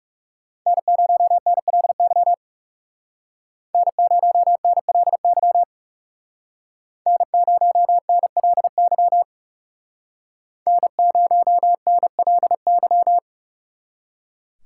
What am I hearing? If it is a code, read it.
Morse code, heard as a keyed tone, about 38 wpm, 700 Hz. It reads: N0NLY N0NLY N0NLY N0NLY